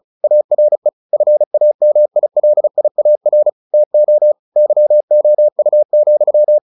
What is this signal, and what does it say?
Morse code, heard as a keyed tone, about 34 wpm, 600 Hz. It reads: ARE FAMILIAR TO YOU,